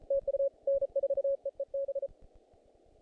Morse code, heard as a keyed tone, about 32 wpm, 560 Hz. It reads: TU N4EEB
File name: contests / naqp-cw-k5tr-op-k5pi- -2000